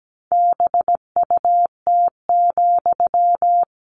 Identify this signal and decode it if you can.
Morse code, heard as a keyed tone, about 17 wpm, 700 Hz. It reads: BUT,
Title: Sentences from Top 1500 Words 17wpm